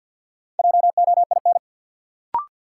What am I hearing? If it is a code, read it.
Morse code, heard as a keyed tone, about 51 wpm, 700 Hz. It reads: JOIN
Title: Sentences from Top 1400 Words - New Word Drill Alt - 50wpm